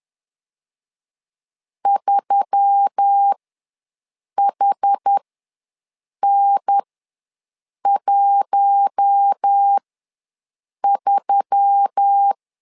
Morse code, heard as a keyed tone, about 10 wpm, spaced as for 6 wpm, 800 Hz. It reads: 3HN13
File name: LEZIONI / 4 Q1S3 ws 10wpm 2437